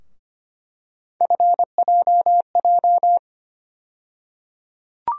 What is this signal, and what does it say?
Morse code, heard as a keyed tone, about 25 wpm, 700 Hz. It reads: FJJ
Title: USA Call Sign Postfixes - 3 Letters 25wpm